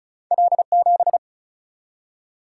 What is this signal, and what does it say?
Morse code, heard as a keyed tone, about 34 wpm, 700 Hz. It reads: L7
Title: Single-Letter Single-Number 35wpm